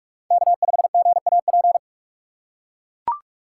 Morse code, heard as a keyed tone, about 43 wpm, 700 Hz. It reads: K5GAP
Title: USA Call Signs - 5 Characters - 6x Character Spacing - 45wpm